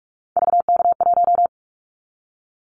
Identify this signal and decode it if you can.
Morse code, heard as a keyed tone, about 46 wpm, 700 Hz. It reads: VK1
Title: Callsign Prefixes for Top 100 DXCC - 45wpm